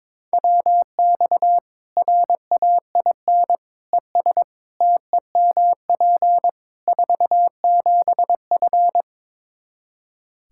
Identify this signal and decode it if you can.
Morse code, heard as a keyed tone, about 22 wpm, 700 Hz. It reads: WX RAIN ES TEMP 47F
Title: Practice QSO - Exchange 2 - Info - 22wpm